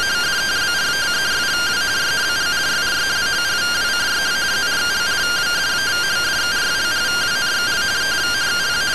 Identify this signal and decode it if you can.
Two-tone teleprinter signal, about 170 Hz shift.
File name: findings / 81 axial iodine 262.gen-04 rtty rx